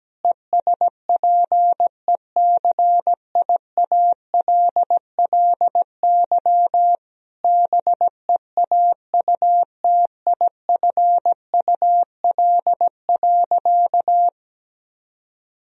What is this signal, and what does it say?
Morse code, heard as a keyed tone, about 17 wpm, 700 Hz. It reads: ESPECIALLY BEAUTIFUL.
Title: Sentences from Top 1000 Words - Review Part 4 - 17wpm